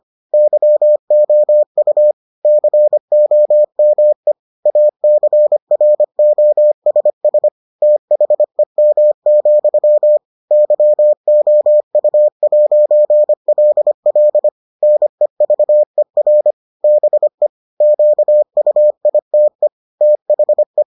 Morse code, heard as a keyed tone, about 25 wpm, 600 Hz. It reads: YOU COME ACROSS THEM, YOU'LL NEVER BE QUITE THE